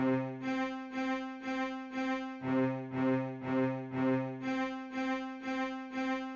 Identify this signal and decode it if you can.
There is no code.